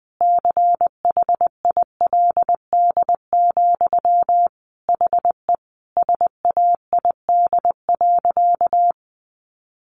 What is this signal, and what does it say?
Morse code, heard as a keyed tone, about 20 wpm, 700 Hz. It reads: CHILD, HE SAID.
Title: Sentences from Top 800 Words 20wpm